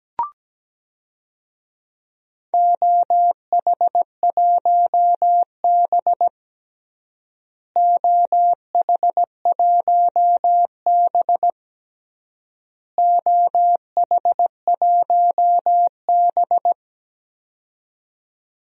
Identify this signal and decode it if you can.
Morse code, heard as a keyed tone, about 17 wpm, 700 Hz. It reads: OH1B OH1B OH1B
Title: DXCC-100 Call Signs - 4 Characters - Repeated 3x - 17wpm